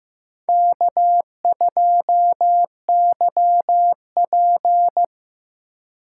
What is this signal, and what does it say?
Morse code, heard as a keyed tone, about 15 wpm, 700 Hz. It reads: K2YP
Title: USA Call Signs - 4 Characters - 4x Character Spacing - 15wpm